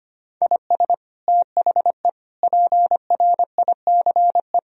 Morse code, heard as a keyed tone, about 25 wpm, 700 Hz. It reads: IS THE PRICE
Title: Sentences from Top 1000 Words - Review Part 4 - 25wpm